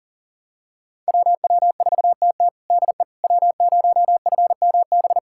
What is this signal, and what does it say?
Morse code, heard as a keyed tone, about 37 wpm, 700 Hz. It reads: WW4TT DE W0FMB
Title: USA Callsign DE Callsign - 5 Characters - 40wpm